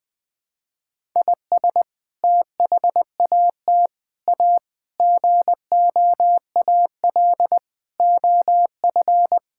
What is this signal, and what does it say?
Morse code, heard as a keyed tone, about 21 wpm, 700 Hz. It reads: IS THAT A GOAL OF
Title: Sentences from Top 1000 Words - Review Part 5 - 20wpm